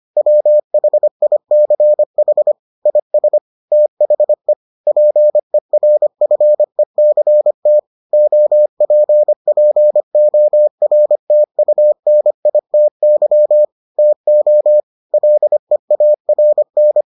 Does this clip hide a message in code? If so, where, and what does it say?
Yes: Morse code from 0.2 to 17.0 s, heard as a keyed tone, about 25 wpm, 600 Hz. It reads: WHICH IS THE PERFECT OPPORTUNITY TO LEARN